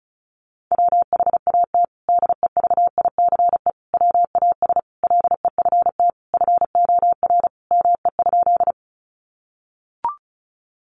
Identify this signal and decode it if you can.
Morse code, heard as a keyed tone, about 34 wpm, 700 Hz. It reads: WHAT DEVICE WAS LEFT FOR ME?
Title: Sentences from Top 1500 Words 35wpm